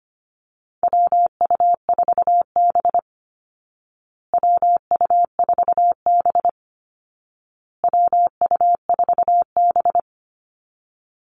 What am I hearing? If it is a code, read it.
Morse code, heard as a keyed tone, about 25 wpm, 700 Hz. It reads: WU4B WU4B WU4B
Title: Call Signs Repeated 2x - 25wpm